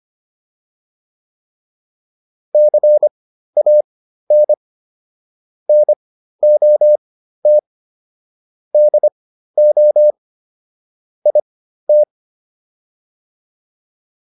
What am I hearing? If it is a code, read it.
Morse code, heard as a keyed tone, about 25 wpm, spaced as for 13 wpm, 600 Hz. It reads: CAN NOT DO IT